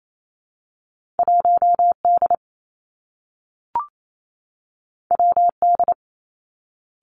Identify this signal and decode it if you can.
Morse code, heard as a keyed tone, about 28 wpm, 700 Hz. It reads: 1 D W D
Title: Two Challenging Letter-Number Combinations 28wpm